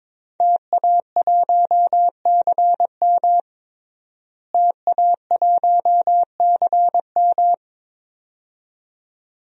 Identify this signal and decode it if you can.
Morse code, heard as a keyed tone, about 22 wpm, 700 Hz. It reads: TA1CM TA1CM
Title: DXCC-100 Call Signs - 5 Characters - Repeated 2x - 22wpm